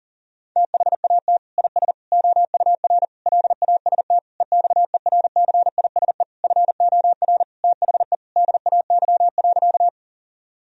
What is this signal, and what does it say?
Morse code, heard as a keyed tone, about 40 wpm, 700 Hz. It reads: THAT IS OUR LAST EXERCISE FOR THE DAY.
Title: Sentences from Top 1000 Words - Review Part 4 - 40wpm